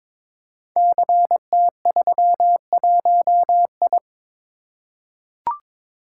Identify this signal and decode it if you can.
Morse code, heard as a keyed tone, about 22 wpm, 700 Hz. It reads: CT31I
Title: Code Groups - 5 Letter:Numbers - 22wpm